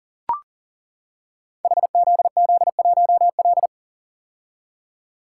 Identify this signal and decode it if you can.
Morse code, heard as a keyed tone, about 39 wpm, 700 Hz. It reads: HZZ1L
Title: Code Groups - 5 Letter:Numbers - 40wpm